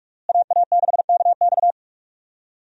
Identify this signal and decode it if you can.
Morse code, heard as a keyed tone, about 44 wpm, 700 Hz. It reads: AA6KX
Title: USA Call Signs - 5 Characters - Repeated 2x - 45wpm